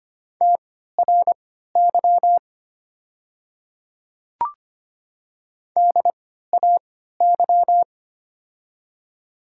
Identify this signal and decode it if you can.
Morse code, heard as a keyed tone, about 25 wpm, spaced as for 14 wpm, 700 Hz. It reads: TRY DAY
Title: 3x Character Spacing - Top 200 Words - 25wpm